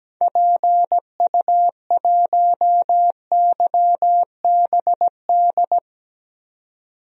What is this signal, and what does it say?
Morse code, heard as a keyed tone, about 17 wpm, 700 Hz. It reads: PU1YBD
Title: DXCC-100 Call Signs - 6 Characters - 17wpm